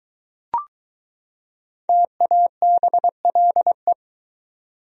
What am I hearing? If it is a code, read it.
Morse code, heard as a keyed tone, about 21 wpm, 700 Hz. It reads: TABLE